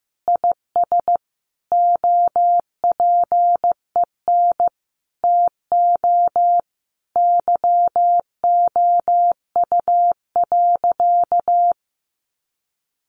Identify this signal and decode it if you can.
Morse code, heard as a keyed tone, about 15 wpm, 700 Hz. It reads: IS OPEN TO YOU.
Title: Sentences from Top 1000 Words 15wpm